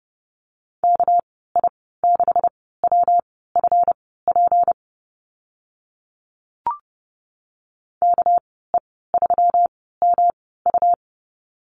Morse code, heard as a keyed tone, about 30 wpm, spaced as for 17 wpm, 700 Hz. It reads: KI6WFP KE3MU
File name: Lesson 37g - Letter x - Call Signs 30wpm